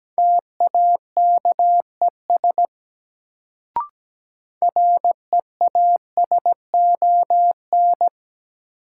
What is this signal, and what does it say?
Morse code, heard as a keyed tone, about 17 wpm, 700 Hz. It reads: TAKES REASON
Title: Sets of 5 Words Encore - New Word Drill - 17wpm